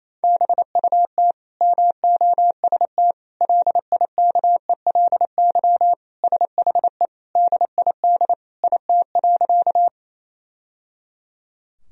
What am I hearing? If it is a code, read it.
Morse code, heard as a keyed tone, about 28 wpm, 700 Hz. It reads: BUT MOST LIKELY SHE DID IT.
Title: Sentences from Top 1000 Words - Review Part 4 - 28wpm